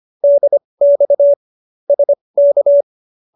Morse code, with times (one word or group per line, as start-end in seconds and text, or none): 0.2-1.3 DX
1.9-2.8 SK